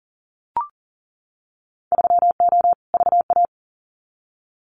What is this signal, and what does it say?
Morse code, heard as a keyed tone, about 40 wpm, 700 Hz. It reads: 3O VA
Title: Winter Field Day Exchanges 40wpm